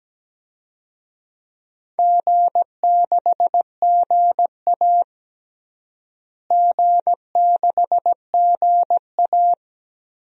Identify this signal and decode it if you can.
Morse code, heard as a keyed tone, about 17 wpm, 700 Hz. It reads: G6GA G6GA
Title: DXCC-100 Call Signs - 4 Characters - Repeated 3x - 17wpm